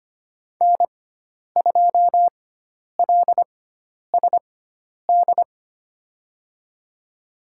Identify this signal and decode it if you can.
Morse code, heard as a keyed tone, about 25 wpm, spaced as for 10 wpm, 700 Hz. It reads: N2LSD
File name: Lesson 16g - Letter c - Call Signs 25wpm